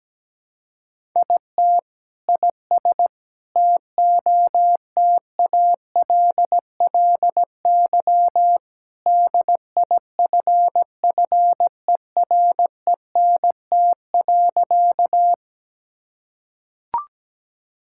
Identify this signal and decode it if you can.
Morse code, heard as a keyed tone, about 17 wpm, 700 Hz. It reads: IT IS TOTALLY DIFFERENT.
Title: Sentences from Top 1600 Words 17wpm